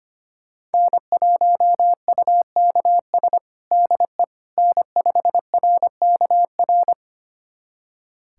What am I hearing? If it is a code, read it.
Morse code, heard as a keyed tone, about 25 wpm, 700 Hz. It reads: N1UKS DE N5RKR
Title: USA Callsign DE Callsign - 5 Characters - 25wpm